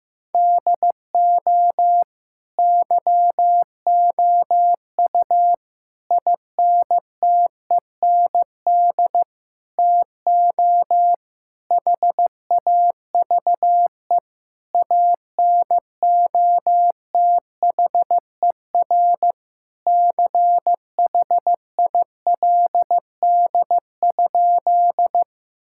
Morse code, heard as a keyed tone, about 15 wpm, 700 Hz. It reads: DO YOU INTEND TO HAVE ANOTHER CHILD?
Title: Sentences from Top 1600 Words 15wpm